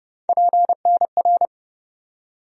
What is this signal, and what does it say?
Morse code, heard as a keyed tone, about 29 wpm, 700 Hz. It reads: PNR
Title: USA Call Sign Postfixes - 3 Letters 30wpm